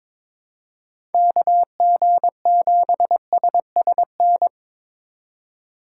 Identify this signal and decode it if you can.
Morse code, heard as a keyed tone, about 22 wpm, 700 Hz. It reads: KG7SSN
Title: USA Call Signs - 6 Characters - 2x Character Spacing - 22wpm